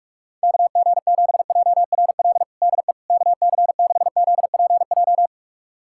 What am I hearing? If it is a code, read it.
Morse code, heard as a keyed tone, about 41 wpm, 700 Hz. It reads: KG7JRL DE KC6ZPJ
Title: USA Callsign DE Callsign - 6 Characters - 45wpm